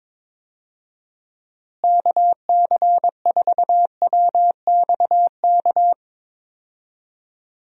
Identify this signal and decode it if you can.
Morse code, heard as a keyed tone, about 22 wpm, 700 Hz. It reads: KC4WXK
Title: USA Call Signs - 6 Characters - Repeated 3x - 22wpm